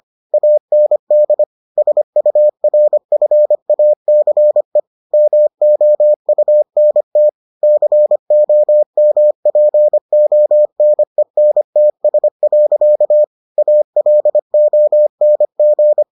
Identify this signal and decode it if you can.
Morse code, heard as a keyed tone, about 25 wpm, 600 Hz. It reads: AND SURFACE MOUNT COMPONENTS. ALONG